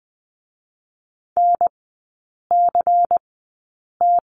Morse code, heard as a keyed tone, about 20 wpm, spaced as for 8 wpm, 700 Hz. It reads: NCT
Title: 121 Lesson 20e - Number 3 - Three Characters 20wpm